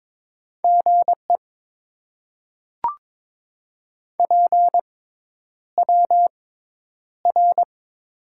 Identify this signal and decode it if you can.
Morse code, heard as a keyed tone, about 22 wpm, spaced as for 10 wpm, 700 Hz. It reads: GE PWR
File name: Top 90 QSO Elements - 6x Character Spacing - 22wpm